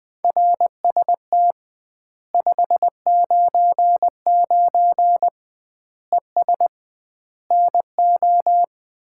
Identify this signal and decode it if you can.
Morse code, heard as a keyed tone, about 21 wpm, 700 Hz. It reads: RST 599 ES NO
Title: Common QSO Phrases - 2x Word-Spacing - 20wpm